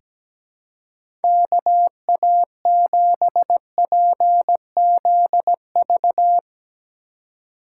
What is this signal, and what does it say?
Morse code, heard as a keyed tone, about 17 wpm, 700 Hz. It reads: KA7PZV
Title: USA Call Signs - 6 Characters - 2x Character Spacing - 17wpm